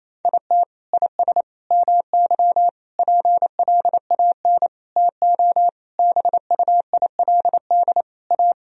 Morse code, heard as a keyed tone, about 28 wpm, 700 Hz. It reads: IT IS MY PLAN TO BUILD A
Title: Sentences from Top 1000 Words - Review Part 2 - 28wpm